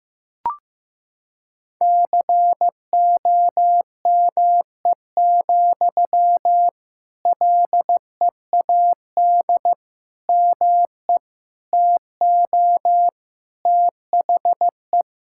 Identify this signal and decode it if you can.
Morse code, heard as a keyed tone, about 16 wpm, 700 Hz. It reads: COME, LEAD ME TO THE
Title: Sentences from Top 1600 Words 15wpm